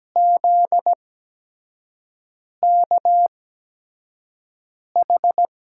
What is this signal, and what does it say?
Morse code, heard as a keyed tone, about 17 wpm, spaced as for 5 wpm, 700 Hz. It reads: ZKH
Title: USA Call Signs - 5 Characters - 8x Character Spacing - 17wpm